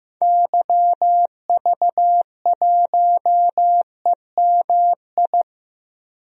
Morse code, heard as a keyed tone, about 15 wpm, 700 Hz. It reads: YV1EMI
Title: DXCC-100 Call Signs - 6 Characters - 15wpm